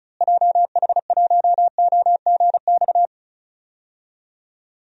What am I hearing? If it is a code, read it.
Morse code, heard as a keyed tone, about 36 wpm, 700 Hz. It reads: JH1OGX
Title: DXCC-100 Call Signs - 6 Characters - Repeated 3x - 35wpm